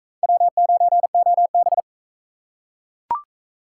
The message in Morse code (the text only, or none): W9OB